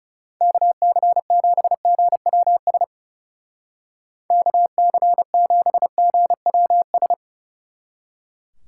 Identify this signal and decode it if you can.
Morse code, heard as a keyed tone, about 36 wpm, 700 Hz. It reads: KC7GWS KC7GWS
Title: Speed Racing - USA 2x3 Call Signs 30wpm